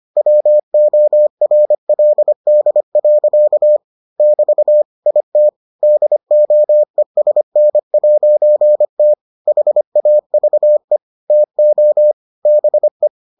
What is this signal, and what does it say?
Morse code, heard as a keyed tone, about 25 wpm, 600 Hz. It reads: WORLD. <BT> IT DOESN'T HAVE TO BE